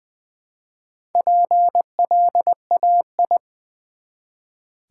Morse code, heard as a keyed tone, about 20 wpm, 700 Hz. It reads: PLAI
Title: Build A Word - 1 Letter At A Time - 5-Letter Words - 20wpm